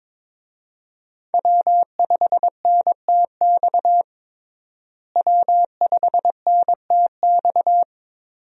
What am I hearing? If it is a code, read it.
Morse code, heard as a keyed tone, about 22 wpm, 700 Hz. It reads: W5NTX W5NTX
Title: USA Call Signs - 5 Characters - Repeated 2x - 22wpm